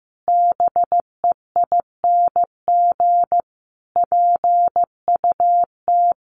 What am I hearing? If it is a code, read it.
Morse code, heard as a keyed tone, about 15 wpm, 700 Hz. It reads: BEING PUT
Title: Sentences from Top 1300 Words 15wpm